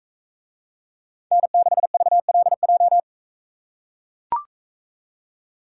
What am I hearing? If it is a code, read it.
Morse code, heard as a keyed tone, about 42 wpm, 700 Hz. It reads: N6VLJ